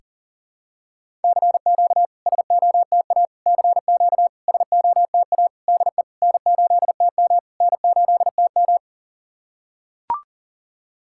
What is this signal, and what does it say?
Morse code, heard as a keyed tone, about 39 wpm, 690 Hz. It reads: CQ SOTA CQ SOTA DE N8TM N8TM
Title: SOTA QSO Practice 40wpm